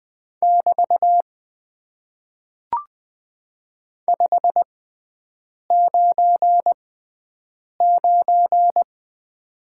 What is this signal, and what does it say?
Morse code, heard as a keyed tone, about 20 wpm, spaced as for 7 wpm, 700 Hz. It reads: <BT> 599
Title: Top 90 QSO Elements - 6x Character Spacing - 20wpm